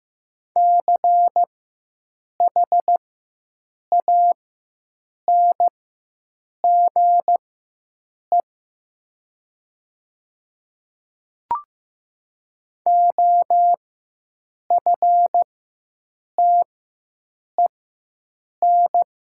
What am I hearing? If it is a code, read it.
Morse code, heard as a keyed tone, about 15 wpm, spaced as for 7 wpm, 700 Hz. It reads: CHANGE OFTEN
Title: 4x Character Spacing - Top 300 Words - 15wpm